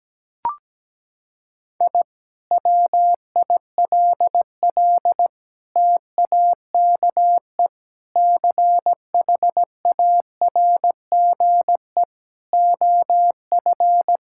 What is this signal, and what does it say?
Morse code, heard as a keyed tone, about 17 wpm, 700 Hz. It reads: I WILL TAKE CHARGE OF
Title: Sentences from Top 1000 Words - Review Part 6 - 17wpm